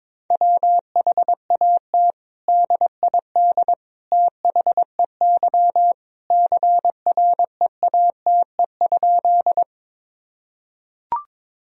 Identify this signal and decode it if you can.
Morse code, heard as a keyed tone, about 21 wpm, 700 Hz. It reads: WHAT DID THEY CREATE?
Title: Sentences from Top 1000 Words - Review Part 3 - 22wpm